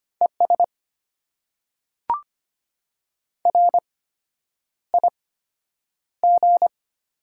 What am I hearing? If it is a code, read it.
Morse code, heard as a keyed tone, about 25 wpm, spaced as for 9 wpm, 700 Hz. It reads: ES RIG
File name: Top 90 QSO Elements - 8x Character Spacing - 25wpm